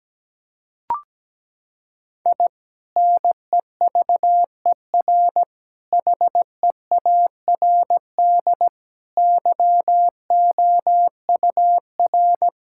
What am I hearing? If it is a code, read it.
Morse code, heard as a keyed tone, about 17 wpm, 700 Hz. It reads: I NEVER HEARD YOUR
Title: Sentences from Top 1000 Words - Review Part 6 - 17wpm